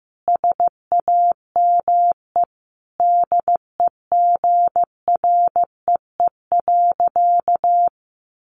Morse code, heard as a keyed tone, about 15 wpm, 700 Hz. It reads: SAME DEGREE.
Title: Sentences from Top 1500 Words 15wpm